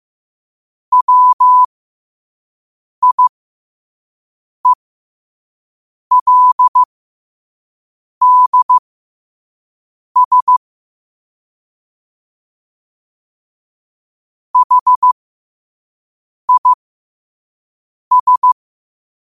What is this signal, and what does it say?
Morse code, heard as a keyed tone, about 15 wpm, spaced as for 5 wpm, 1000 Hz. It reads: WIELDS HIS
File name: qotdfiles / 5wpm